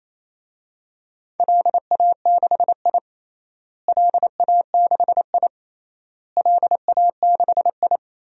Morse code, heard as a keyed tone, about 28 wpm, 700 Hz. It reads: LA6I LA6I LA6I
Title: DXCC-100 Call Signs - 4 Characters - Repeated 3x - 28wpm